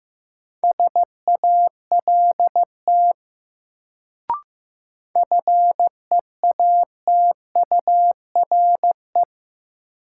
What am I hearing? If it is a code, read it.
Morse code, heard as a keyed tone, about 15 wpm, 700 Hz. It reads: SALT FEATURE
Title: Sentences from Top 800 Words - New Word Drill - 15wpm